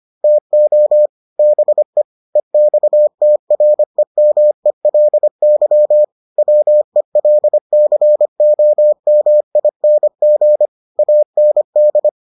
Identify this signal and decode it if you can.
Morse code, heard as a keyed tone, about 25 wpm, 600 Hz. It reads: TO BE EXTREMELY WELCOMING AND